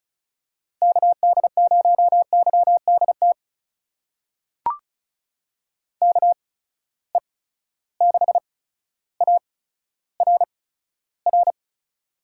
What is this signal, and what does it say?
Morse code, heard as a keyed tone, about 36 wpm, spaced as for 15 wpm, 700 Hz. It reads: KD0YDT KE6ARR
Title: USA Call Signs - 6 Characters - 8x Character Spacing - 35wpm